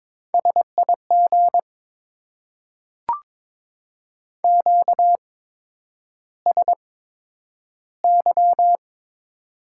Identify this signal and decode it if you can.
Morse code, heard as a keyed tone, about 22 wpm, spaced as for 9 wpm, 700 Hz. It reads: SIG QSY
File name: Top 90 QSO Elements - 8x Character Spacing - 22wpm